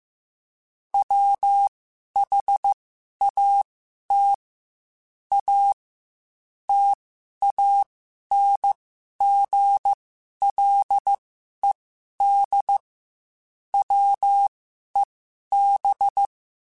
Morse code, heard as a keyed tone, about 15 wpm, spaced as for 11 wpm, 800 Hz. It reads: WHAT A TANGLED WEB